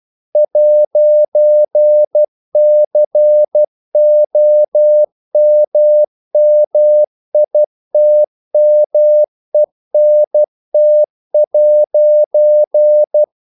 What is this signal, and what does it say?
Morse code, heard as a keyed tone, about 12 wpm, 600 Hz. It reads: 'COMMITMENT'
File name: CW / Practice / EnglishQuotations0112